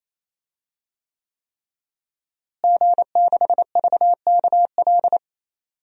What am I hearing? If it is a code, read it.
Morse code, heard as a keyed tone, about 28 wpm, 700 Hz. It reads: G6VKL
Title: DXCC-100 Call Signs - 5 Characters - Repeated 3x - 28wpm